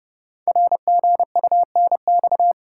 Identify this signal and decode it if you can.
Morse code, heard as a keyed tone, about 30 wpm, 690 Hz. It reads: RGUNX
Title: Code Groups - 5 Letters - 30wpm